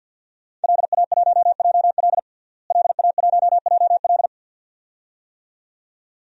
Morse code, heard as a keyed tone, about 48 wpm, 700 Hz. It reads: LA1JL LA1JL
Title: DXCC-100 Call Signs - 5 Characters - Repeated 2x - 50wpm